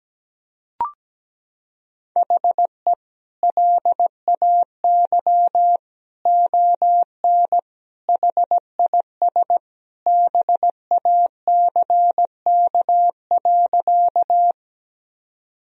Morse code, heard as a keyed tone, about 17 wpm, 700 Hz. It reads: HE LAY ON HIS BACK.
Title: Sentences from Top 1000 Words - Review Part 2 - 17wpm